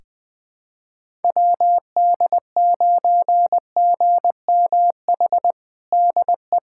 Morse code, heard as a keyed tone, about 20 wpm, 700 Hz. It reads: WD9GMH DE
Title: USA Callsign DE Callsign 20wpm